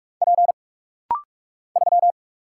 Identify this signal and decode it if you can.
Morse code, heard as a keyed tone, about 43 wpm, 700 Hz. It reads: P 3
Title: Single Letter-Number Alt - Rapid-Fire 45wpm